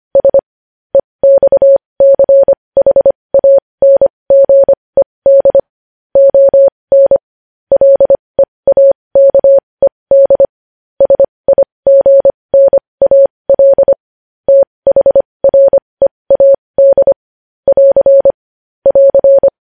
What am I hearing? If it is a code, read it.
Morse code, heard as a keyed tone, about 25 wpm, spaced as for 20 wpm, 550 Hz. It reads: S EXCHANGED ON LEAKED SIGNAL THREAD <AR> <AR>
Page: News Headlines in Morse Code at 20 WPM